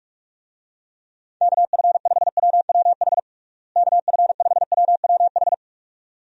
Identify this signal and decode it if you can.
Morse code, heard as a keyed tone, about 46 wpm, 700 Hz. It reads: KF5WWH KF5WWH
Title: USA Call Signs - 6 Characters - Repeated 2x - 45wpm